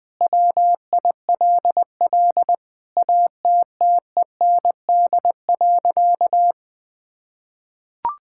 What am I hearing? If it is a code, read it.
Morse code, heard as a keyed tone, about 20 wpm, 700 Hz. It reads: WILL ATTEND.
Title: Sentences with 7-Word Phrases - Spoken Beginning - 20wpm